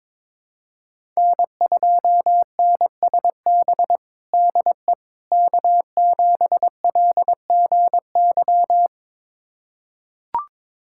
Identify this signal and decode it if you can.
Morse code, heard as a keyed tone, about 22 wpm, 700 Hz. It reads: N2NSB DE K7LGY
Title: USA Callsign DE Callsign - 5 Characters - 22wpm